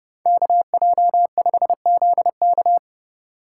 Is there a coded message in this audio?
Yes: Morse code from 0.3 to 2.8 s, heard as a keyed tone, about 29 wpm, 700 Hz. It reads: KJ5ZK